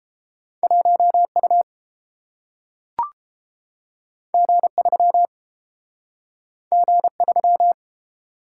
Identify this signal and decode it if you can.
Morse code, heard as a keyed tone, about 30 wpm, 700 Hz. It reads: 1U G3 G3